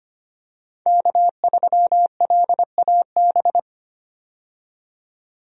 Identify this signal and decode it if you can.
Morse code, heard as a keyed tone, about 25 wpm, 700 Hz. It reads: K3LAB
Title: USA Call Signs - 5 Characters - 25wpm